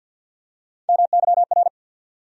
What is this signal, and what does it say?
Morse code, heard as a keyed tone, about 49 wpm, 700 Hz. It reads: MYR
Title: USA Call Sign Postfixes - 3 Letters 50wpm